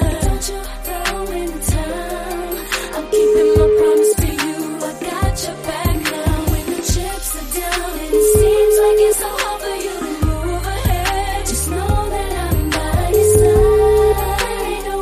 Ringback tone (Europe, 425 Hz), with noise.